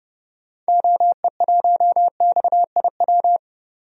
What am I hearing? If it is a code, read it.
Morse code, heard as a keyed tone, about 30 wpm, 700 Hz. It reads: OE1XIW
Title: DXCC-100 Call Signs - 6 Characters - 30wpm